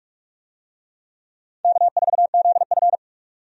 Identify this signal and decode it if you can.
Morse code, heard as a keyed tone, about 46 wpm, 700 Hz. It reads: K4ZF
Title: USA Call Signs - 4 Characters - 45wpm